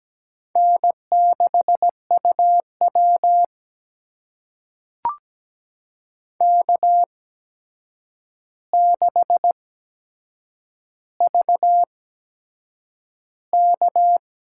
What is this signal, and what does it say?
Morse code, heard as a keyed tone, about 17 wpm, spaced as for 7 wpm, 700 Hz. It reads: N6UW K6VK
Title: USA Call Signs - 4 Characters - 8x Character Spacing - 17wpm